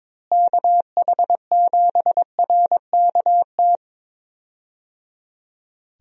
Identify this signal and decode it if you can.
Morse code, heard as a keyed tone, about 22 wpm, 700 Hz. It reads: KH7RKT
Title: USA Call Signs - 6 Characters - Repeated 3x - 22wpm